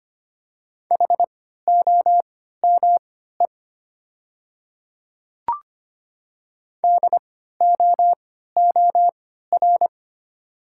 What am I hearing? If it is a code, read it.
Morse code, heard as a keyed tone, about 25 wpm, spaced as for 14 wpm, 700 Hz. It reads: HOME DOOR
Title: 3x Character Spacing - Top 300 Words - 25wpm